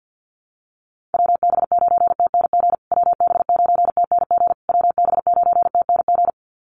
Morse code, heard as a keyed tone, about 49 wpm, 700 Hz. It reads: WB9TNG WB9TNG WB9TNG
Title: USA Call Signs Repeated 3x 50wpm